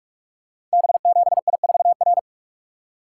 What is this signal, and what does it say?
Morse code, heard as a keyed tone, about 44 wpm, 700 Hz. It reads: B7I4R
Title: Code Groups - 5 Letter:Numbers - 45wpm